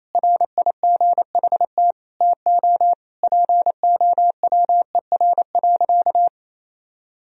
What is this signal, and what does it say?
Morse code, heard as a keyed tone, about 28 wpm, 700 Hz. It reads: RIGHT TO POWER.
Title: Sentences from Top 1000 Words - Review Part 2 - 28wpm